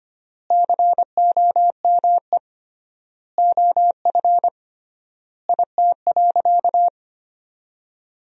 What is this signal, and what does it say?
Morse code, heard as a keyed tone, about 25 wpm, 700 Hz. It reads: COME OF IT.
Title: 3x Word-Spacing - Sentences from Top 500 Words - 25wpm